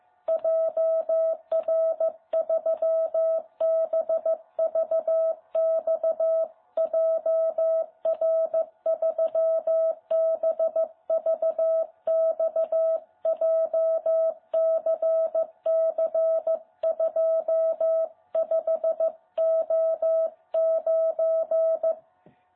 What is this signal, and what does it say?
Morse code, heard as a keyed tone, about 14 wpm, 660 Hz. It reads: JR3BVX JR3BVX JCC25O9
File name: jh8cbh / 2021cw